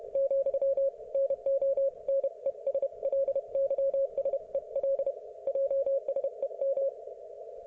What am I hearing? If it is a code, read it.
Morse code, heard as a keyed tone, about 31 wpm, 530 Hz. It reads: , NO NESLYSEL JSEM